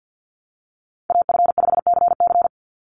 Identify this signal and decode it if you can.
Morse code, heard as a keyed tone, about 48 wpm, 700 Hz. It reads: AF5CC
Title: Call Signs Repeated 3x - 50wpm